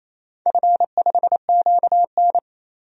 Morse code, heard as a keyed tone, about 27 wpm, 700 Hz. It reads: F5QN
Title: DXCC-100 Call Signs - 4 Characters - 28wpm